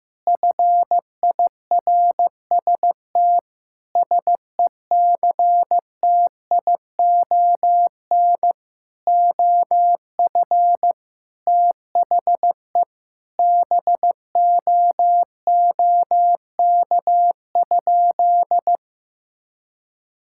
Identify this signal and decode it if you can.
Morse code, heard as a keyed tone, about 15 wpm, 700 Hz. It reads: FIRST SECTION OF THE BOOK?
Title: Sentences from Top 1000 Words - Review Part 5 - 15wpm